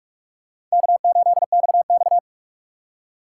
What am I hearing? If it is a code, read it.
Morse code, heard as a keyed tone, about 46 wpm, 700 Hz. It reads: K8XX
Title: USA Call Signs - 4 Characters - 6x Character Spacing - 45wpm